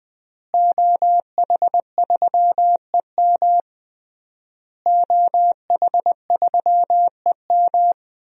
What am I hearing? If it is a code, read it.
Morse code, heard as a keyed tone, about 21 wpm, 700 Hz. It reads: OH3EM OH3EM
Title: DXCC-100 Call Signs - 5 Characters - Repeated 3x - 20wpm